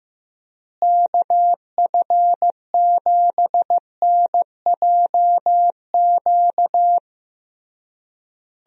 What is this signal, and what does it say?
Morse code, heard as a keyed tone, about 15 wpm, 700 Hz. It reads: KF7NJQ
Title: USA Call Signs - 6 Characters - Repeated 3x - 15wpm